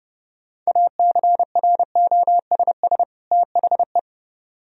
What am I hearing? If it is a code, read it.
Morse code, heard as a keyed tone, about 29 wpm, 700 Hz. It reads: ACROSS THE